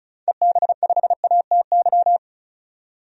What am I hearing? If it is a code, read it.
Morse code, heard as a keyed tone, about 36 wpm, 700 Hz. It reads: EB5ATY
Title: DXCC-100 Call Signs - 6 Characters - 35wpm